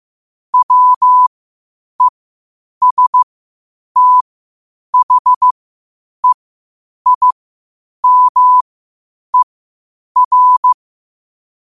Morse code, heard as a keyed tone, about 15 wpm, spaced as for 8 wpm, 1000 Hz. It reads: WESTHEIMER
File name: qotdfiles / 8wpm